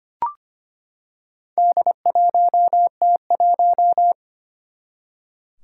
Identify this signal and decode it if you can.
Morse code, heard as a keyed tone, about 25 wpm, 700 Hz. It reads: D1T1
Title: Four Challenging Letter-Number Combinations 25wpm